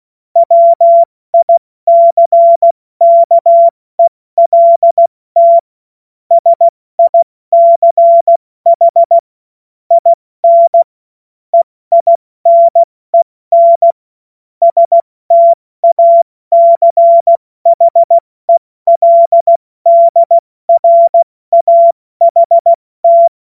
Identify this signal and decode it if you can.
Morse code, heard as a keyed tone, about 16 wpm, spaced as for 14 wpm, 680 Hz. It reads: WICKELT SICH IN EINEN STACHELDRAHT
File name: Morsekurs / Witze / Witz 26 Dezember 2023 70 F80